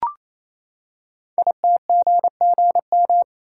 Morse code, heard as a keyed tone, about 29 wpm, 700 Hz. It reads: ITGGM
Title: Code Groups - 5 Letters - 28wpm